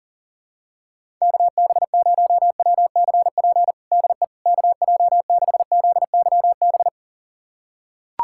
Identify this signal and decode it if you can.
Morse code, heard as a keyed tone, about 40 wpm, 700 Hz. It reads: KB0WCP DE KJ6ZYB
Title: USA Callsign DE Callsign - 6 Characters - 40wpm